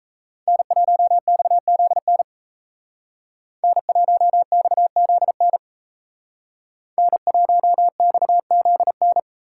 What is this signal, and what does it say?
Morse code, heard as a keyed tone, about 36 wpm, 700 Hz. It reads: N1XZN N1XZN N1XZN